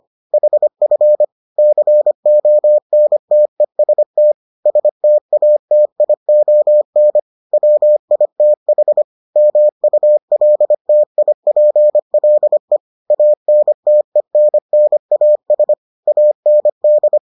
Morse code, heard as a keyed tone, about 25 wpm, 600 Hz. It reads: HF CONTEST STATION WITH MULTIPLE ANTENNAS AND